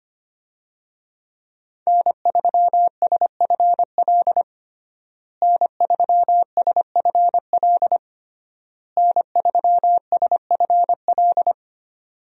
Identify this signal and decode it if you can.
Morse code, heard as a keyed tone, about 25 wpm, 700 Hz. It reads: N3SFL N3SFL N3SFL
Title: USA Call Signs - 5 Characters - Repeated 3x - 25wpm